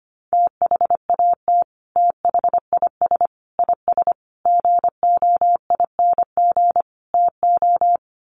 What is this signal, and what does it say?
Morse code, heard as a keyed tone, about 25 wpm, 700 Hz. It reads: THAT THIS IS GOING TO
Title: Sets of 5 Words Encore2 25wpm